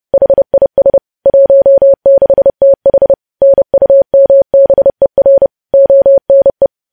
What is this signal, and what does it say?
Morse code, heard as a keyed tone, about 30 wpm, 550 Hz. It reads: HIS 16TH NUMBER ONE